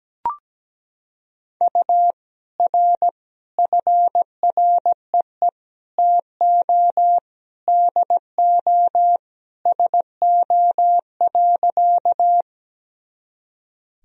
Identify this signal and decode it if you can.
Morse code, heard as a keyed tone, about 17 wpm, 700 Hz. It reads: U R FREE TO DO SO.
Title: Sentences from Top 500 Words and Abbreviations 17wpm